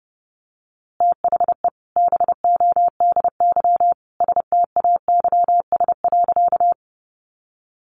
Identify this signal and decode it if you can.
Morse code, heard as a keyed tone, about 30 wpm, 700 Hz. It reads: THE BODY STAYS.
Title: A Sound Of Thunder 30wpm